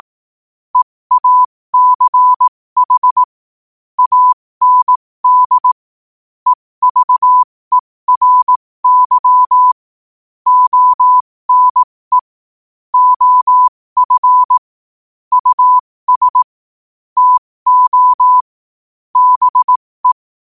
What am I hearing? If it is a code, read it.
Morse code, heard as a keyed tone, about 18 wpm, spaced as for 15 wpm, 1000 Hz. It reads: EACH AND EVERY ONE OF US TO BE
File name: qotdfiles / 15wpm